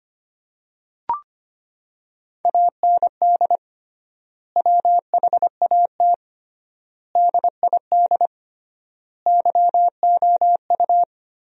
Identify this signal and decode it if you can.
Morse code, heard as a keyed tone, about 25 wpm, 700 Hz. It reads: AND WHAT DID YOU
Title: 3x Word-Spacing - Sentences from Top 100 Words - 25wpm